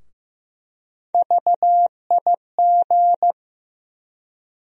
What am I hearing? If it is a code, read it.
Morse code, heard as a keyed tone, about 15 wpm, 700 Hz. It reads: VIG
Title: USA Call Sign Postfixes - 3 Letters 15wpm